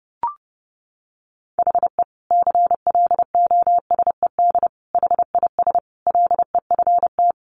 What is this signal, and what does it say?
Morse code, heard as a keyed tone, about 29 wpm, 700 Hz. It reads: HE CLOSED HIS LEFT
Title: Sentences from Top 1400 Words 30wpm